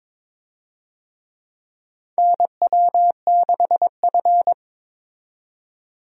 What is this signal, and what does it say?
Morse code, heard as a keyed tone, about 22 wpm, 700 Hz. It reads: NW6F
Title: USA Call Signs - 4 Characters - 22wpm